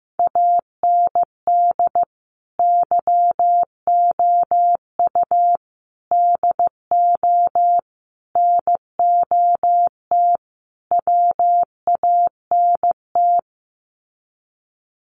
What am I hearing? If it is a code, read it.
Morse code, heard as a keyed tone, about 15 wpm, 700 Hz. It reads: AND YOU DO NOT WANT
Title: Sets of 5 Words Encore2 15wpm